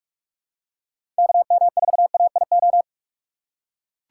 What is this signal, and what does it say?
Morse code, heard as a keyed tone, about 43 wpm, 700 Hz. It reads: KM4AIO
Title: USA Call Signs - 6 Characters - 4x Character Spacing - 45wpm